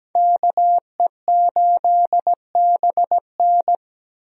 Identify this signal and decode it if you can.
Morse code, heard as a keyed tone, about 17 wpm, 700 Hz. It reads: KE8BN
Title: USA Call Signs - 5 Characters - Repeated 3x - 17wpm